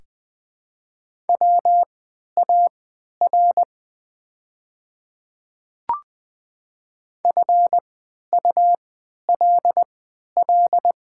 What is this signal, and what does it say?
Morse code, heard as a keyed tone, about 20 wpm, spaced as for 11 wpm, 700 Hz. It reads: WAR FULL
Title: 3x Character Spacing - Top 500 Words - 20wpm